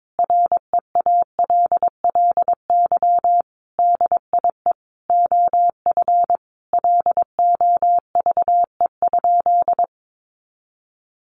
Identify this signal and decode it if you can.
Morse code, heard as a keyed tone, about 22 wpm, 700 Hz. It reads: REALLY DIE OF LOVE?
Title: Sentences from Top 1300 Words 22wpm